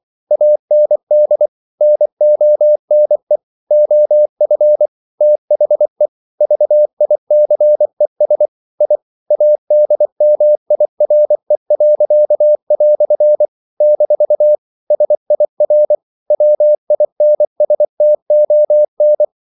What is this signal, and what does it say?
Morse code, heard as a keyed tone, about 24 wpm, 600 Hz. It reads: AND NONE OF THE VICES I ADMIRE." - SIR WINSTON
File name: CW / Practice / EnglishQuotations0424